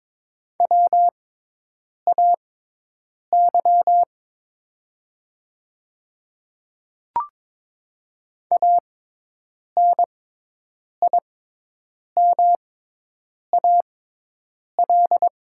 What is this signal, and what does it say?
Morse code, heard as a keyed tone, about 22 wpm, spaced as for 8 wpm, 700 Hz. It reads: WAY ANIMAL
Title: 6x Character Spacing - Top 200 Words - 22wpm